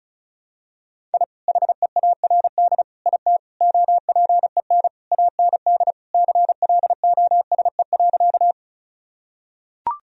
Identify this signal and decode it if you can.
Morse code, heard as a keyed tone, about 36 wpm, 690 Hz. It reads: I HEARD IT OPEN AND CLOSE.
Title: Sentences from Top 1000 Words - Review Part 1 - 35wpm